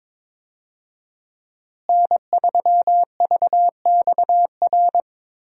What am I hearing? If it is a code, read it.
Morse code, heard as a keyed tone, about 22 wpm, 700 Hz. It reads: N3VXR
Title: USA Call Signs - 5 Characters - Repeated 3x - 22wpm